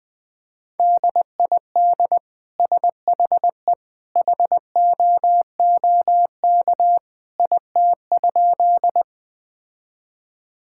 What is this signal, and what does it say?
Morse code, heard as a keyed tone, about 20 wpm, 700 Hz. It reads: DID SHE HOOK IT?
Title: Sentences from Top 1600 Words 20wpm